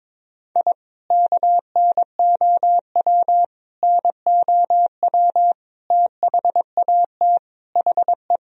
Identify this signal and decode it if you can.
Morse code, heard as a keyed tone, about 22 wpm, 700 Hz. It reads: I KNOW NOW THAT HE
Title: Sentences from Top 1000 Words - Review Part 1 - 22wpm